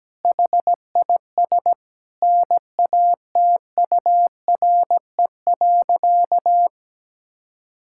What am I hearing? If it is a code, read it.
Morse code, heard as a keyed tone, about 17 wpm, 700 Hz. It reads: HIS NATURE.
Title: Sentences from Top 1000 Words - Review Part 5 - 17wpm